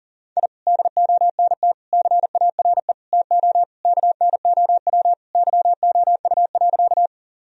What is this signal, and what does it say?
Morse code, heard as a keyed tone, about 40 wpm, 700 Hz. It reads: I DONT CARE TO KNOW YOU.
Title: Sentences from Top 1000 Words - Review Part 1 - 40wpm